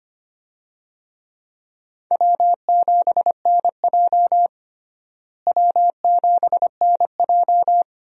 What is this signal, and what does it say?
Morse code, heard as a keyed tone, about 25 wpm, 700 Hz. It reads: W7NJ W7NJ
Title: USA Call Signs - 4 Characters - Repeated 2x - 25wpm